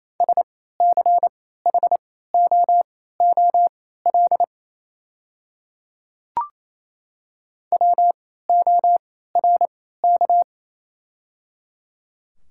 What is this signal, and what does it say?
Morse code, heard as a keyed tone, about 28 wpm, spaced as for 16 wpm, 700 Hz. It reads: SCHOOL WORK
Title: 3x Character Spacing - Top 300 Words - 28wpm